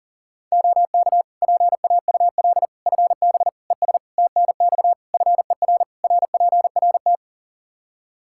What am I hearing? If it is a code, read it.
Morse code, heard as a keyed tone, about 40 wpm, 700 Hz. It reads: OK PAUL FB ES TNX FER RPRT
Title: K1USN SST QSO Practice 40wpm